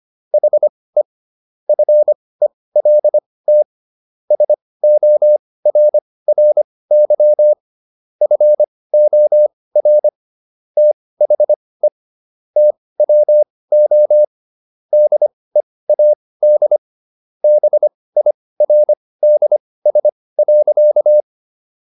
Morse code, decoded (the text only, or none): HE FELT SORRY FOR THE TWO DEAD BIRDS.